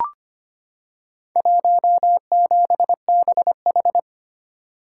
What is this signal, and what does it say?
Morse code, heard as a keyed tone, about 25 wpm, 700 Hz. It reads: 17BH
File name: practice / 25wpm / Four Letter-Number Combinations - 25wpm